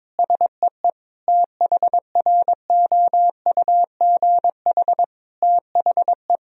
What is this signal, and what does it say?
Morse code, heard as a keyed tone, about 22 wpm, 700 Hz. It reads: SEE THROUGH THE
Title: Sentences from Top 1000 Words - Review Part 1 - 22wpm